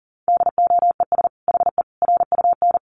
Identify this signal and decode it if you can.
Morse code, heard as a keyed tone, about 40 wpm, 700 Hz. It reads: DOES HE RUN
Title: Sentences from Top 1300 Words 40wpm